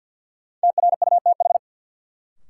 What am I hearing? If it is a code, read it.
Morse code, heard as a keyed tone, about 46 wpm, 700 Hz. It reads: TRUTH
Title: Sets of 5 Words - New Word Drill - 50wpm